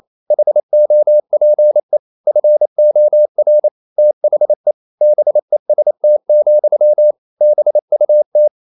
Morse code, heard as a keyed tone, about 28 wpm, 600 Hz. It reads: HOPE FOR THE BEST, BUT
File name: CW / Practice / EnglishSayings28